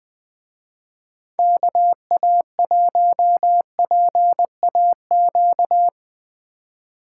Morse code, heard as a keyed tone, about 20 wpm, 700 Hz. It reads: KA1PAQ